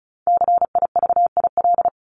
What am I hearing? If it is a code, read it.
Morse code, heard as a keyed tone, about 34 wpm, 700 Hz. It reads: CIVIL
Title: Top 1751-2000 Words 35wpm